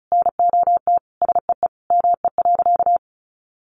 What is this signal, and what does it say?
Morse code, heard as a keyed tone, about 34 wpm, 700 Hz. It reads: NOT SEE ME.
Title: Sentences from Top 600 Words 35wpm